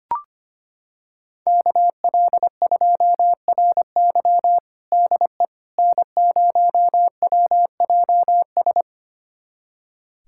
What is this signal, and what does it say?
Morse code, heard as a keyed tone, about 25 wpm, 700 Hz. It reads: KL2RY DE N0WJS
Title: USA Callsign DE Callsign - 5 Characters - 25wpm